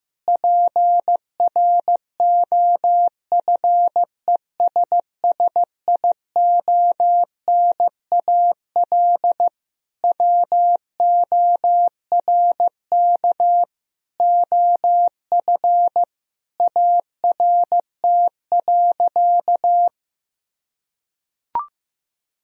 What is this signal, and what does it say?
Morse code, heard as a keyed tone, about 16 wpm, 700 Hz. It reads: PROFESSIONAL WORK OF ART.
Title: Sentences from Top 1000 Words - Review Part 4 - 15wpm